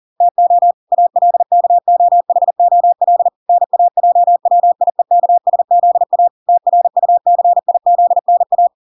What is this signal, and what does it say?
Morse code, heard as a keyed tone, about 40 wpm, 700 Hz. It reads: TO ALKOHOL NAJWIEKSZA TRUCIZNA